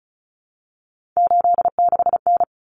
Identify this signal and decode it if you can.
Morse code, heard as a keyed tone, about 34 wpm, 700 Hz. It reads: 86N
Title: Common Cut Numbers - 3-Digit 35wpm